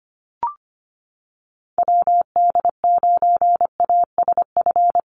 Morse code, heard as a keyed tone, about 25 wpm, 700 Hz. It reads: WD9ASF
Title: USA Call Signs Repeated 2x 25wpm